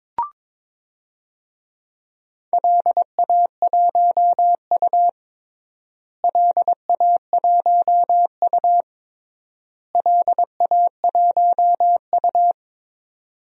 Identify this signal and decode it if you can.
Morse code, heard as a keyed tone, about 21 wpm, 700 Hz. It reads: LA1U LA1U LA1U
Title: DXCC-100 Call Signs - 4 Characters - Repeated 3x - 22wpm